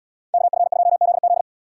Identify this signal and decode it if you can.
Morse code, heard as a keyed tone, about 73 wpm, 700 Hz. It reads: KF2ZZ